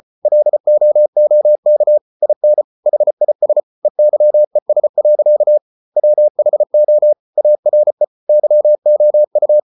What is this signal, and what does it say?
Morse code, heard as a keyed tone, about 34 wpm, 600 Hz. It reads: LOOK IN HIS EYES. WHO ARE YOU